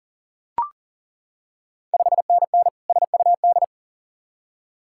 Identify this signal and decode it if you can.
Morse code, heard as a keyed tone, about 40 wpm, 690 Hz. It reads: 5NN SUD
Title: Ontario QSO Party Exchange 40wpm